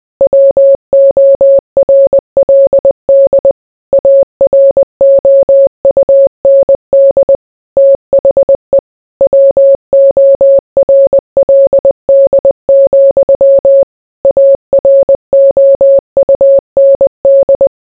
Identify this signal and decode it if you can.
Morse code, heard as a keyed tone, about 20 wpm, 550 Hz. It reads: WORLD AROUND THE WORLD, AROUND